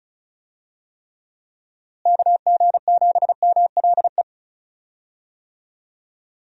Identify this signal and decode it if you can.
Morse code, heard as a keyed tone, about 34 wpm, 700 Hz. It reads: KG7MLE
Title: USA Call Signs - 6 Characters - Repeated 2x - 35wpm